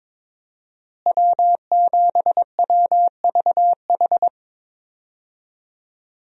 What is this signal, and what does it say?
Morse code, heard as a keyed tone, about 22 wpm, 700 Hz. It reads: W7WVH
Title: USA Call Signs - 5 Characters - Repeated 2x - 22wpm